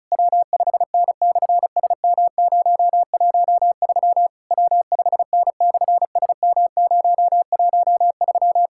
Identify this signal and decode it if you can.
Morse code, heard as a keyed tone, about 36 wpm, 700 Hz. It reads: W5N/SM013 W5N/SM013
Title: SOTA QSO Practice 35wpm